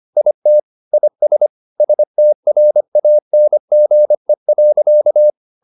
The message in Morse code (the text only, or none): IT IS STRANGE.